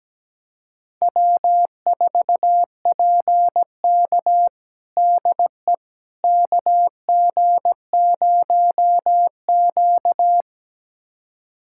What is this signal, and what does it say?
Morse code, heard as a keyed tone, about 17 wpm, 700 Hz. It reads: W4PK DE KG0Q
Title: USA Callsign DE Callsign - 4 Characters - 17wpm